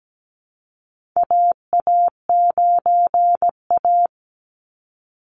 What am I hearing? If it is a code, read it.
Morse code, heard as a keyed tone, about 17 wpm, 700 Hz. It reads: AA9A
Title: Call Signs Repeated 2x - 17wpm